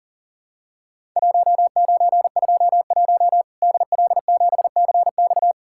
Morse code, heard as a keyed tone, about 39 wpm, 700 Hz. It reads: 1921 DL7CX
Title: Worked All Europe DX Contest - QTC Exchange - 40wpm